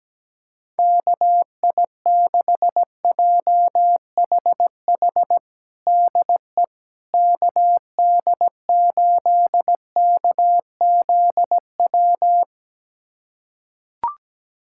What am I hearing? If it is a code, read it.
Morse code, heard as a keyed tone, about 17 wpm, 700 Hz. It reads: KI6JHH DE KD8KZW
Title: USA Callsign DE Callsign 17wpm